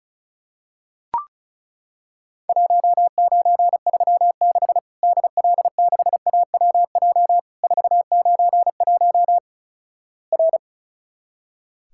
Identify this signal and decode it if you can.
Morse code, heard as a keyed tone, about 36 wpm, 700 Hz. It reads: 1936 DL6AWJ 491 R
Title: Worked All Europe DX Contest - QTC Exchange - 35wpm